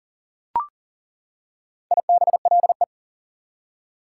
Morse code, heard as a keyed tone, about 40 wpm, 700 Hz. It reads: IBLE
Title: Prefixes and Suffixes 40wpm